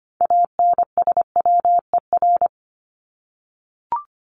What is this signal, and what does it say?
Morse code, heard as a keyed tone, about 25 wpm, 700 Hz. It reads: ANSWER
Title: Top 500 Words 25wpm